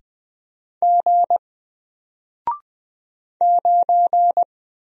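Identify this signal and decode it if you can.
Morse code, heard as a keyed tone, about 20 wpm, 700 Hz. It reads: G 9
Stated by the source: Single letter and number Morse drill at 20 wpm, from N4LCD.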